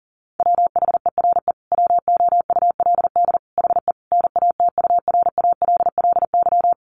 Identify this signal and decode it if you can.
Morse code, heard as a keyed tone, about 40 wpm, 700 Hz. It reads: WHERE WOULD HE NATURALLY
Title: Sentences from Top 1300 Words 40wpm